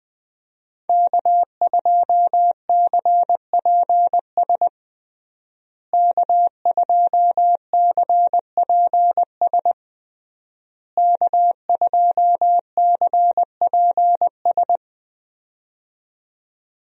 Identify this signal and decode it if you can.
Morse code, heard as a keyed tone, about 20 wpm, 700 Hz. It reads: K2CPS K2CPS K2CPS
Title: USA Call Signs - 5 Characters - Repeated 3x - 20wpm